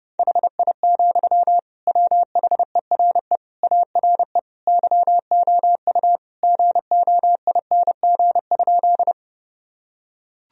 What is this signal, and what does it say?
Morse code, heard as a keyed tone, about 30 wpm, 700 Hz. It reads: HI, WHERE ARE YOU GOING?
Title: Sentences from Top 1000 Words - Review Part 3 - 30wpm